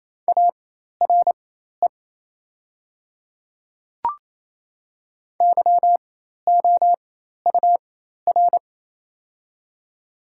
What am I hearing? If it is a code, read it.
Morse code, heard as a keyed tone, about 28 wpm, spaced as for 13 wpm, 700 Hz. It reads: ARE YOUR
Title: 4x Character Spacing - Top 100 Words - 28wpm